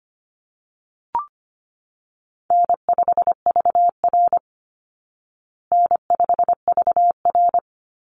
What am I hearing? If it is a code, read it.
Morse code, heard as a keyed tone, about 25 wpm, 700 Hz. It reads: N5VR N5VR
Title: Easy Call Signs Repeated 2x - 25wpm